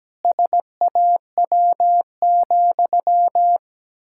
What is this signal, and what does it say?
Morse code, heard as a keyed tone, about 17 wpm, 700 Hz. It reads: SAW,
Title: Pangram Challenge v2 - 17wpm